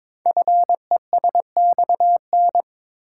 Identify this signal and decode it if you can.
Morse code, heard as a keyed tone, about 22 wpm, 700 Hz. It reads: FESXN
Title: Code Groups - 5 Letters - 22wpm